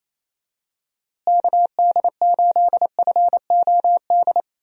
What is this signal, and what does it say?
Morse code, heard as a keyed tone, about 28 wpm, 700 Hz. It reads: KD8FOD
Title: USA Call Signs - 6 Characters - Repeated 2x - 28wpm